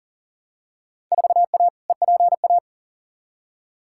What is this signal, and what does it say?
Morse code, heard as a keyed tone, about 40 wpm, 700 Hz. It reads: 4A EPA
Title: Field Day Exchanges - 40wpm